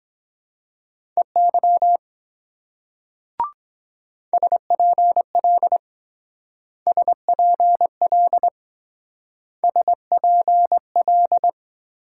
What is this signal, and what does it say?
Morse code, heard as a keyed tone, about 21 wpm, 700 Hz. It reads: EY SPL SPL SPL
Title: Speed-Racing Multi-letter Phonemes - 17wpm